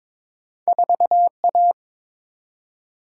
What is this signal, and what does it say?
Morse code, heard as a keyed tone, about 22 wpm, 700 Hz. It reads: 4A
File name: practice / 22wpm / Two Letter-Number Combinations - 22wpm